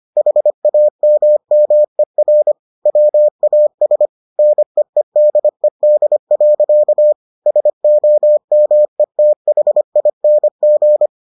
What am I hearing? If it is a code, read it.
Morse code, heard as a keyed tone, about 25 wpm, 600 Hz. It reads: HAMMER WAS NEEDED. SOMETHING